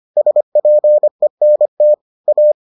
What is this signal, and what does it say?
Morse code, heard as a keyed tone, about 25 wpm, 600 Hz. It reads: SPENT A